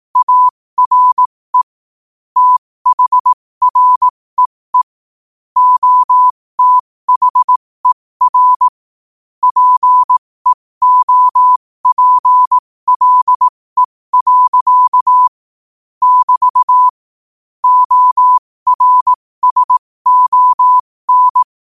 Morse code, heard as a keyed tone, about 18 wpm, spaced as for 15 wpm, 1000 Hz. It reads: ARE THREE OTHER PEOPLE. <BT> ORSON
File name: qotdfiles / 15wpm